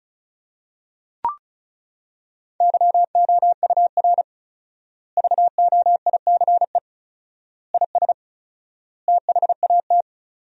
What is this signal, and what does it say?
Morse code, heard as a keyed tone, about 36 wpm, 700 Hz. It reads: YOUR VOICE IS THAT
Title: 4x Word-Spacing - Sentences from Top 500 Words - 35wpm